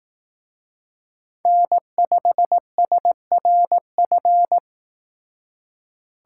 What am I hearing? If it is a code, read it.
Morse code, heard as a keyed tone, about 18 wpm, 700 Hz. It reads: N5SRF
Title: Speed Racing - USA Call Signs 15wpm